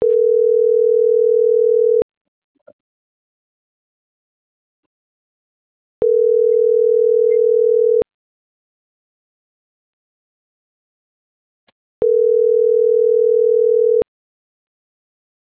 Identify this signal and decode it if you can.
Ringback tone (North America), 440 + 480 Hz.